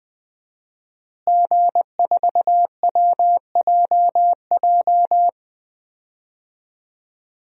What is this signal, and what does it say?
Morse code, heard as a keyed tone, about 20 wpm, 700 Hz. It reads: G4WJJ
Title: DXCC-100 Call Signs - 5 Characters - Repeated 3x - 20wpm